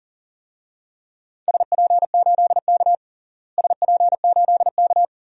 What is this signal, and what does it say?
Morse code, heard as a keyed tone, about 40 wpm, 700 Hz. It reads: SP8K SP8K
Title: DXCC-100 Call Signs - 4 Characters - Repeated 2x - 40wpm